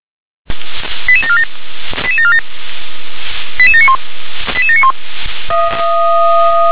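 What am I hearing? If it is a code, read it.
Five-tone selective call (ZVEI), heard as quick runs of five tones, tones of about 80 ms. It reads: 88747 88747 88791 88791
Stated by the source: ZVEI five-tone call received by SDR.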